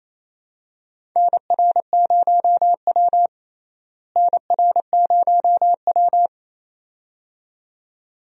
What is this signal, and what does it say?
Morse code, heard as a keyed tone, about 28 wpm, 700 Hz. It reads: NR0W NR0W
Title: USA Call Signs - 4 Characters - Repeated 2x - 28wpm